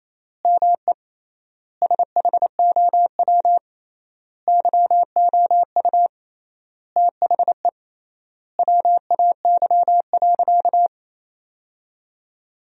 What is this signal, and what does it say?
Morse code, heard as a keyed tone, about 28 wpm, 700 Hz. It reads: ME SHOW YOU THE WAY.
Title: 3x Word-Spacing - Sentences from Top 300 Words - 28wpm